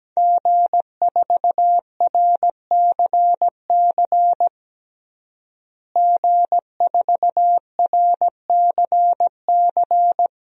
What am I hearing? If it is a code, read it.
Morse code, heard as a keyed tone, about 17 wpm, 700 Hz. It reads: G4RCC G4RCC
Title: DXCC-100 Call Signs - 5 Characters - Repeated 2x - 17wpm